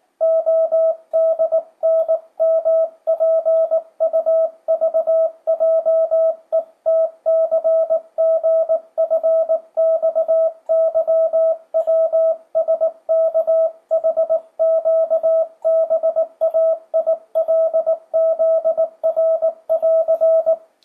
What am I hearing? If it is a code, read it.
Morse code, heard as a keyed tone, about 18 wpm, 650 Hz, with radio noise. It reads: ODNMPUVJETCGFXYWSKHQBAILZR<AR>
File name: 2025cw / speedup / 19wpm